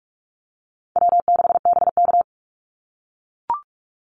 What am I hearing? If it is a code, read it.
Morse code, heard as a keyed tone, about 44 wpm, 700 Hz. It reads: W6BK
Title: Call Signs - 45wpm